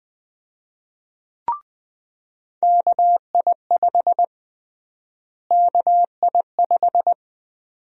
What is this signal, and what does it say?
Morse code, heard as a keyed tone, about 20 wpm, 700 Hz. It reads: KI5 KI5
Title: USA Call Sign Prefixes - Repeated 2x - 20wpm